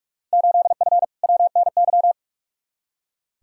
Morse code, heard as a keyed tone, about 41 wpm, 700 Hz. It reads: 8F WNY
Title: Field Day Exchanges Encore - 45wpm